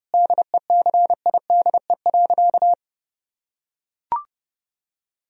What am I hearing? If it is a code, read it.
Morse code, heard as a keyed tone, about 30 wpm, 700 Hz. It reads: DECIDE.
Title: Sentences from Top 1000 Words - Review Part 3 - 30wpm